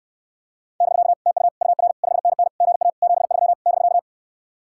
Morse code, heard as a keyed tone, about 67 wpm, 700 Hz. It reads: <BK> TU UR 5NN GA 73 <BK>